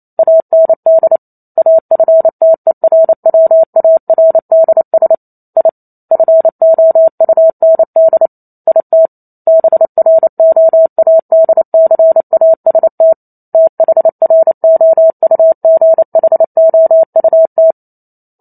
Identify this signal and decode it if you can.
Morse code, heard as a keyed tone, about 28 wpm, 650 Hz. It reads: AND AFTERWARDS I FOUND IT BROADCAST THROUGHOUT
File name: Chapter 2 - 28 WPM